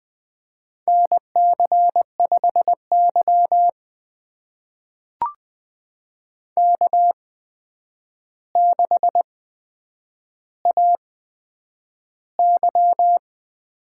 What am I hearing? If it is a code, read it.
Morse code, heard as a keyed tone, about 20 wpm, spaced as for 9 wpm, 700 Hz. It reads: NC5Y K6AY
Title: USA Call Signs - 4 Characters - 8x Character Spacing - 20wpm